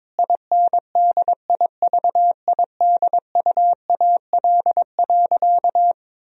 Morse code, heard as a keyed tone, about 22 wpm, 700 Hz. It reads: INDIVIDUAL.
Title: Sentences from Top 1000 Words - Review Part 4 - 22wpm